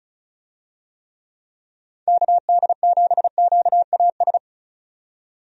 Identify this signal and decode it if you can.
Morse code, heard as a keyed tone, about 36 wpm, 700 Hz. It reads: KD7QAS
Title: USA Call Signs - 6 Characters - 35wpm